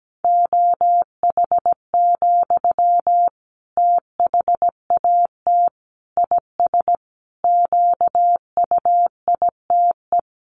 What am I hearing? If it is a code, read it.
Morse code, heard as a keyed tone, about 17 wpm, 700 Hz. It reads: OH, THAT IS QUITE
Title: Sentences from Top 700 Words 17wpm